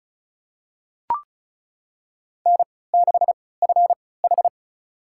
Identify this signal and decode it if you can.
Morse code, heard as a keyed tone, about 36 wpm, spaced as for 20 wpm, 700 Hz. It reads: N6FH
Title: USA Call Signs - 4 Characters - 3x Character Spacing - 35wpm